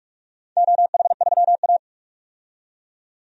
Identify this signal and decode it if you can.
Morse code, heard as a keyed tone, about 43 wpm, 700 Hz. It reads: OH3A
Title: DXCC-100 Call Signs - 4 Characters - 45wpm